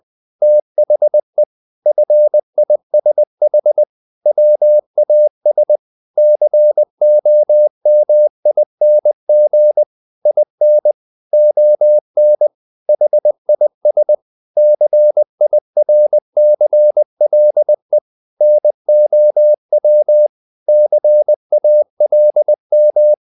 Morse code, heard as a keyed tone, about 20 wpm, 600 Hz. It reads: THE FISH WAS COMING IN ON HIS CIRCLE NOW CALM
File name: the old man and the sea part 2 20wpm